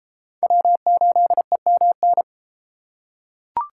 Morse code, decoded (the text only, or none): W8EMN